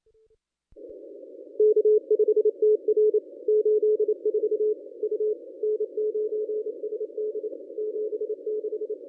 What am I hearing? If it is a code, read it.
Morse code, heard as a keyed tone, about 28 wpm, 440 Hz. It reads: K5TR 84 U N9SD 76
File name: contests / ss-cw-k5tr-op-n5rz -1000